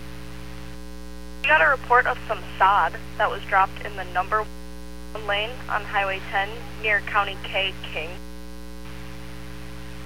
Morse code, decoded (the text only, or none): none